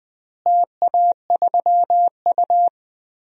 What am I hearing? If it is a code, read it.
Morse code, heard as a keyed tone, about 20 wpm, 700 Hz. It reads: TA3U
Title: DXCC-100 Call Signs - 4 Characters - Repeated 2x - 20wpm